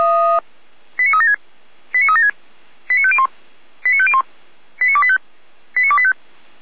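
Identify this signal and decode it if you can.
Five-tone selective call (ZVEI), heard as quick runs of five tones, tones of about 70 ms. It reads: 89376 89376 89501 89501 89285 89285
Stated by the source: ZVEI five-tone selective call received by SDR, fire brigade Orlamünde (feuerwehr-orlamuende).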